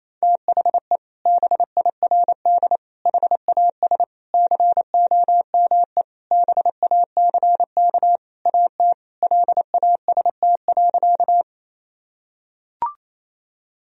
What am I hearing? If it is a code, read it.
Morse code, heard as a keyed tone, about 28 wpm, 700 Hz. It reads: THE BIRD HAS COME BACK AT LAST.
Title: Sentences from Top 1000 Words - Review Part 2 - 28wpm